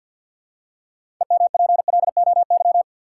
Morse code, heard as a keyed tone, about 53 wpm, 700 Hz. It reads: EMPLOY
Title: Six Letter Words - 50wpm